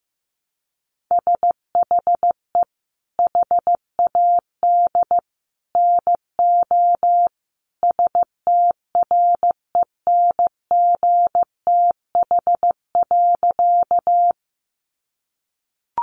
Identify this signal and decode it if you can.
Morse code, heard as a keyed tone, about 14 wpm, 700 Hz. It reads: SHE HAD NO STRENGTH.
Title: Sentences from Top 1300 Words 15wpm